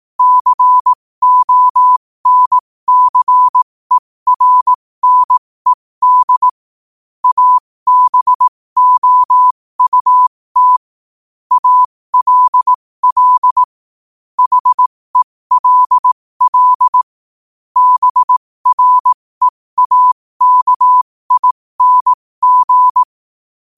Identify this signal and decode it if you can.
Morse code, heard as a keyed tone, about 18 wpm, spaced as for 15 wpm, 1000 Hz. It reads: CONCERNED ABOUT ALL HELL BREAKING
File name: qotdfiles / 15wpm